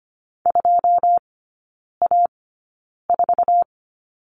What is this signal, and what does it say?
Morse code, heard as a keyed tone, about 25 wpm, spaced as for 9 wpm, 700 Hz. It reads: 2A4
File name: Lesson 17e - Letter u - Three Characters 25wpm